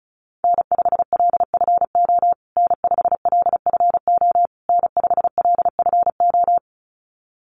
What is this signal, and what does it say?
Morse code, heard as a keyed tone, about 34 wpm, 700 Hz. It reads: N5LFO N5LFO N5LFO
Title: USA Call Signs Repeated 3x 35wpm